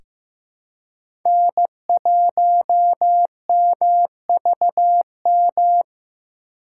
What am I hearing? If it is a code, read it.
Morse code, heard as a keyed tone, about 16 wpm, 700 Hz. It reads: N1MVM
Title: USA Call Signs - 5 Characters - 4x Character Spacing - 15wpm